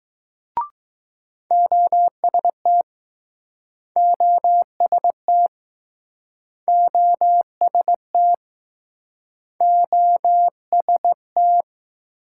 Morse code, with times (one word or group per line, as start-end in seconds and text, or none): 1.5-2.8 OST
4.0-5.5 OST
6.7-8.3 OST
9.6-11.6 OST